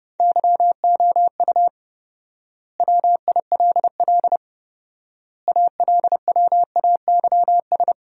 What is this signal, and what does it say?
Morse code, heard as a keyed tone, about 30 wpm, 700 Hz. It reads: YOU WILL ALWAYS
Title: 4x Word-Spacing - Sentences from Top 400 Words - 30wpm